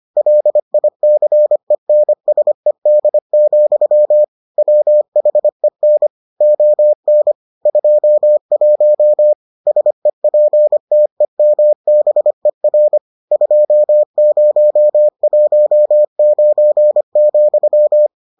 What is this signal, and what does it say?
Morse code, heard as a keyed tone, about 25 wpm, 600 Hz. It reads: LICENSED, WHEN ON 21 SEPTEMBER 2019,